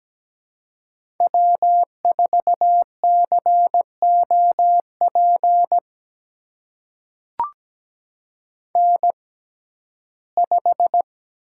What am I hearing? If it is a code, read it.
Morse code, heard as a keyed tone, about 17 wpm, spaced as for 12 wpm, 700 Hz. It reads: W4COP N5
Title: USA Call Signs - 5 Characters - 6x Character Spacing - 17wpm